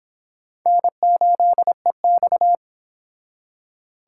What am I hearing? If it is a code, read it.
Morse code, heard as a keyed tone, about 26 wpm, 700 Hz. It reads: N8EX